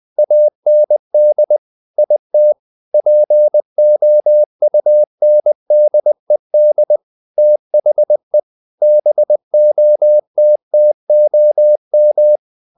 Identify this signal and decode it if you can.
Morse code, heard as a keyed tone, about 20 wpm, 600 Hz. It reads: AND IT POUNDED THE BOTTOM
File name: the old man and the sea part 2 20wpm